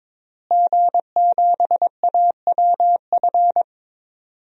Morse code, heard as a keyed tone, about 22 wpm, 700 Hz. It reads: G7AWF
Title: DXCC-100 Call Signs - 5 Characters - Repeated 2x - 22wpm